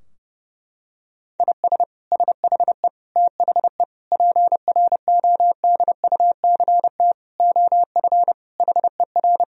Morse code, heard as a keyed tone, about 30 wpm, 700 Hz. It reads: IS SHE THE PRODUCT OF HER
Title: Sentences from Top 1000 Words - Review Part 2 - 30wpm